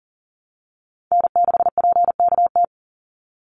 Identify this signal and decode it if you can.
Morse code, heard as a keyed tone, about 40 wpm, 700 Hz. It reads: N6PKT
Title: NA SOTA Chaser Callsigns 40wpm